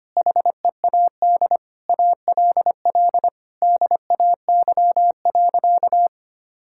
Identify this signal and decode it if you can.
Morse code, heard as a keyed tone, about 25 wpm, 700 Hz. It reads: HEAD ALL DAY.
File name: practice / 25wpm / Difficult Sentences with Hints - 25wpm